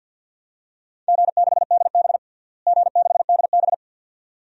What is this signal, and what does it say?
Morse code, heard as a keyed tone, about 52 wpm, 700 Hz. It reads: G6DB G6DB
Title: DXCC-100 Call Signs - 4 Characters - Repeated 2x - 50wpm